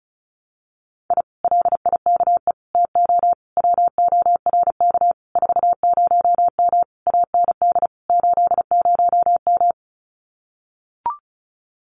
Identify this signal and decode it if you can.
Morse code, heard as a keyed tone, about 34 wpm, 700 Hz. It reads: I LIKE TO WORK 40M AND 80M
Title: Common QSO Phrases Encore 35wpm